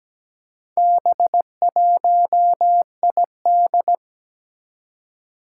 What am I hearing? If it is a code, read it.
Morse code, heard as a keyed tone, about 17 wpm, 700 Hz. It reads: B1ID
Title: Four Challenging Letter-Number Combinations 17wpm